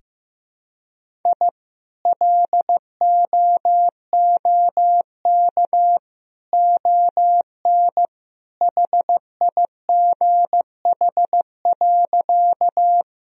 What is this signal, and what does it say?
Morse code, heard as a keyed tone, about 15 wpm, 700 Hz. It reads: I LOOK ON HIGH.
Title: Sentences from Top 1000 Words - Review Part 1 - 15wpm